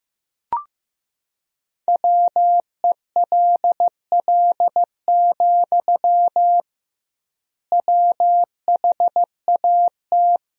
Morse code, heard as a keyed tone, about 15 wpm, 700 Hz. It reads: WELL, WHAT
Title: 2x Word-Spacing - Sentences from Top 200 Words - 15wpm